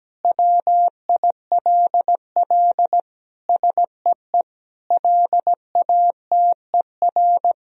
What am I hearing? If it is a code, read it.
Morse code, heard as a keyed tone, about 17 wpm, 700 Hz. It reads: WILL SEE LATER
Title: Sentences from Top 1000 Words - Review Part 4 - 17wpm